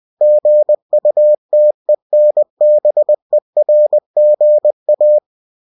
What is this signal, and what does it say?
Morse code, heard as a keyed tone, about 20 wpm, 600 Hz. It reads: GUTENBERGA
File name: cw-books / winnie-the-pooh / winnie-20wpm-page-0113